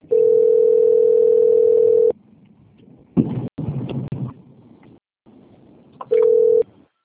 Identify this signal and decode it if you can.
Ringback tone (North America), 440 + 480 Hz.